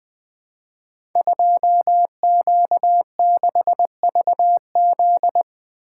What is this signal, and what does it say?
Morse code, heard as a keyed tone, about 20 wpm, 700 Hz. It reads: 2Q6VZ
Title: Code Groups - 5 Letter:Numbers - 20wpm